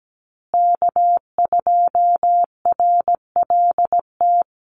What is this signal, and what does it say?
Morse code, heard as a keyed tone, about 17 wpm, 700 Hz. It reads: K2RLT
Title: USA Call Signs Repeated 2x 17wpm